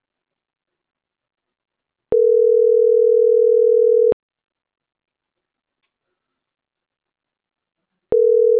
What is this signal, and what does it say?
Ringback tone (North America), 440 + 480 Hz.